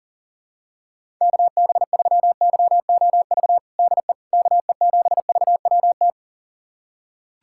Morse code, heard as a keyed tone, about 41 wpm, 690 Hz. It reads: KB3YOV DE KE7VWT
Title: USA Callsign DE Callsign - 6 Characters - 40wpm